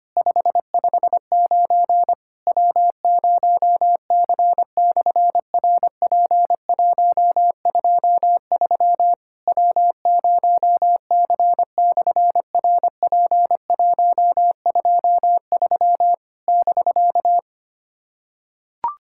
Morse code, heard as a keyed tone, about 24 wpm, 700 Hz. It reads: 559 W0C/RP123 W0C/RP123 <BK>
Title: SOTA QSO Practice 25wpm